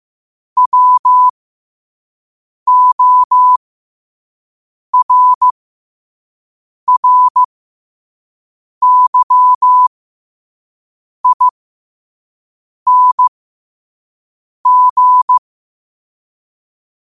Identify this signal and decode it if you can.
Morse code, heard as a keyed tone, about 15 wpm, spaced as for 5 wpm, 1000 Hz. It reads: WORRYING
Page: New quotes every day in morse code at 5 Words per minute.